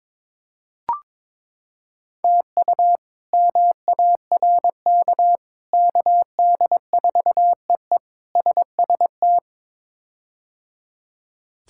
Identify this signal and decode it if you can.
Morse code, heard as a keyed tone, about 22 wpm, 700 Hz. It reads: TU MARK KD4EE SST
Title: K1USN SST QSO Practice - 22wpm